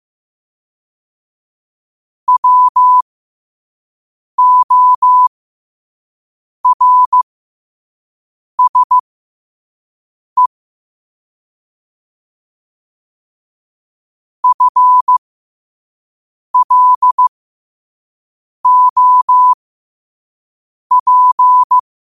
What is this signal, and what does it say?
Morse code, heard as a keyed tone, about 15 wpm, spaced as for 5 wpm, 1000 Hz. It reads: WORSE FLOP
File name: qotdfiles / 5wpm